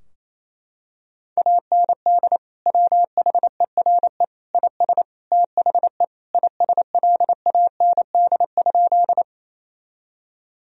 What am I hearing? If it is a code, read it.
Morse code, heard as a keyed tone, about 28 wpm, 700 Hz. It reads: AND WHERE IS THE ISLAND?
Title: Sentences from Top 1000 Words - Review Part 3 - 28wpm